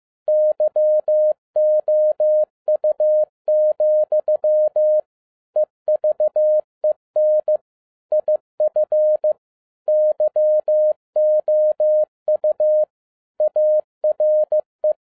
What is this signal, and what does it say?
Morse code, heard as a keyed tone, about 15 wpm, 610 Hz. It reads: YOU, EVEN IF YOU ARE
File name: 2012-01-mp3 / n17-SUMMARY OF FEA NET IN 2011-15wpm